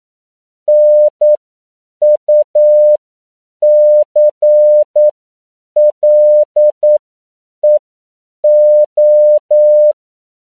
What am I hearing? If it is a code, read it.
Morse code, heard as a keyed tone, about 9 wpm, spaced as for 7 wpm, 600 Hz. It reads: NUCLEO
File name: CW / 10PPM